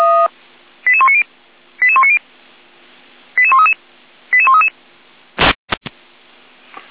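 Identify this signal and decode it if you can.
Five-tone selective call (ZVEI), heard as quick runs of five tones, tones of about 70 ms. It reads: 88190 88190 88133 88133
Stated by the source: ZVEI five-tone selective call received on an SDR.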